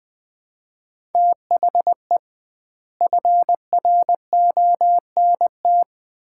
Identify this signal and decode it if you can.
Morse code, heard as a keyed tone, about 20 wpm, 700 Hz. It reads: THE FRONT
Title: 2x Word-Spacing - Sets of 4 Words - 20wpm